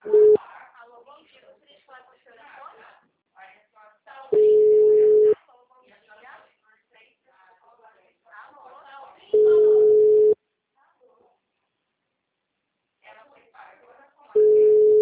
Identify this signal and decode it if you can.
Ringback tone (Europe, 425 Hz), with speech.